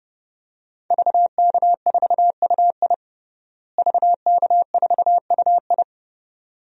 Morse code, heard as a keyed tone, about 30 wpm, 700 Hz. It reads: VK4UI VK4UI
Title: DXCC-100 Call Signs - 5 Characters - Repeated 2x - 30wpm